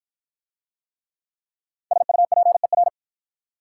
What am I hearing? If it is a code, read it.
Morse code, heard as a keyed tone, about 53 wpm, 700 Hz. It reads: SUPER